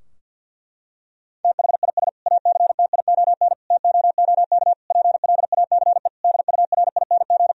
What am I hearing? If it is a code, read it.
Morse code, heard as a keyed tone, about 50 wpm, 700 Hz. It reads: THIS ACTION TOOK PLACE DURING
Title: Sentences from Top 1000 Words - Review Part 4 - 50wpm